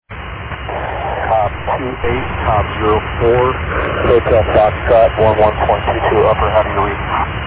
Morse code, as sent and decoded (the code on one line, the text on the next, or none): none